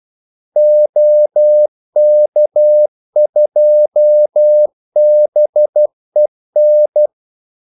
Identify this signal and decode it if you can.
Morse code, heard as a keyed tone, about 12 wpm, 600 Hz. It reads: OK2BEN
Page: 2 ekstra ordmellemrum mellem kaldesignalerne